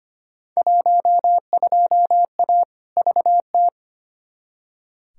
Morse code, heard as a keyed tone, about 25 wpm, 700 Hz. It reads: 12A VT
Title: Field Day Exchanges Encore - 25wpm